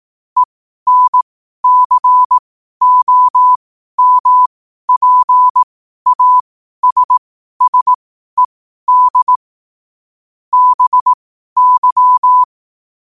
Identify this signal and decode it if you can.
Morse code, heard as a keyed tone, about 18 wpm, spaced as for 12 wpm, 1000 Hz. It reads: ENCOMPASSED BY
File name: qotdfiles / 12wpm